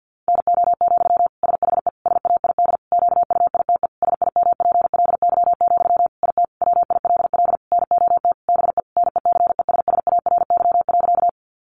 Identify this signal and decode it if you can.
Morse code, heard as a keyed tone, about 50 wpm, 700 Hz. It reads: NO, SHE SAID QUITE SIMPLY, IT WILL NOT BE NECESSARY.
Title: Sentences from Top 800 Words 50wpm